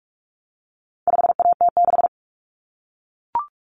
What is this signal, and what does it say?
Morse code, heard as a keyed tone, about 44 wpm, 700 Hz. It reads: 5AT6
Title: Common Cut Numbers - 4-Digit 45wpm